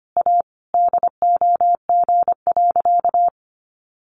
Morse code, heard as a keyed tone, about 25 wpm, 700 Hz. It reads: A DOG.
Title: Additional Easy Phrases 25wpm